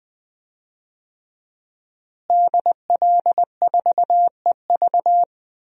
Morse code, heard as a keyed tone, about 20 wpm, 700 Hz. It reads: DL4EV
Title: DXCC-100 Call Signs - 5 Characters - Repeated 3x - 20wpm